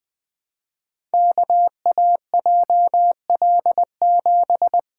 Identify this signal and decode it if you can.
Morse code, heard as a keyed tone, about 20 wpm, 700 Hz. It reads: KAJL7
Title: Code Groups - 5 Letter:Numbers - 20wpm